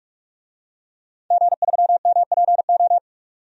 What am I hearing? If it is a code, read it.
Morse code, heard as a keyed tone, about 42 wpm, 700 Hz. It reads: G3MPO
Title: DXCC-100 Call Signs - 5 Characters - 45wpm